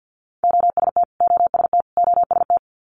Morse code, heard as a keyed tone, about 50 wpm, 690 Hz. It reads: OST OST OST
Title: Multi-letter Phonemes Repeated 3x 50wpm